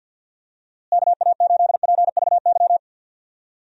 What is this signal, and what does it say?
Morse code, heard as a keyed tone, about 51 wpm, 700 Hz. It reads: KA8PVY
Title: USA Call Signs - 6 Characters - Repeated 3x - 50wpm